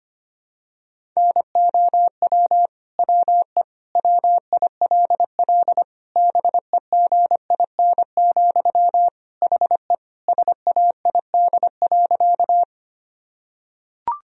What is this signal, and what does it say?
Morse code, heard as a keyed tone, about 25 wpm, 700 Hz. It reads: NOW WE WILL BEGIN, HE SAID.
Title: Sentences from Top 1000 Words - Review Part 1 - 25wpm